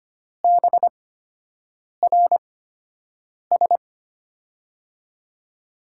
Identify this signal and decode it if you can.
Morse code, heard as a keyed tone, about 25 wpm, spaced as for 7 wpm, 700 Hz. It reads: BRS
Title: USA Call Signs - 5 Characters - 8x Character Spacing - 25wpm